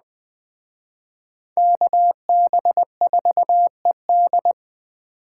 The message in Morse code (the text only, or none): KB4ED